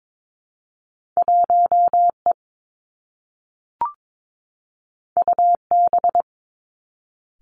Morse code, heard as a keyed tone, about 22 wpm, 700 Hz. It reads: 1 E U B
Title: Two Challenging Letter-Number Combinations 22wpm